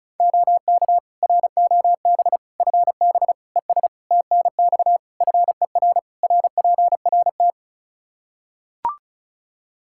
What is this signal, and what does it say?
Morse code, heard as a keyed tone, about 34 wpm, 700 Hz. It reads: OK ROB FB ES TNX FER RPRT
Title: Practice QSO - Exchange 2 - Info - 35wpm